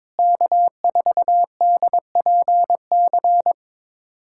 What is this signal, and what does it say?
Morse code, heard as a keyed tone, about 22 wpm, 700 Hz. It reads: K4DPC
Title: USA Callsign DE Callsign - 5 Characters - 22wpm